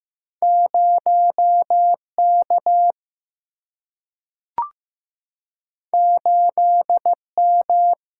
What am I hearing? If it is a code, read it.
Morse code, heard as a keyed tone, about 15 wpm, 700 Hz. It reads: 0K 8M
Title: Single-Letter Single-Number 15wpm